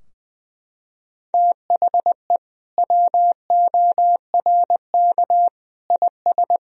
Morse code, heard as a keyed tone, about 21 wpm, 700 Hz. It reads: THE WORK IS
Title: Sentences from Top 1000 Words - Review Part 4 - 20wpm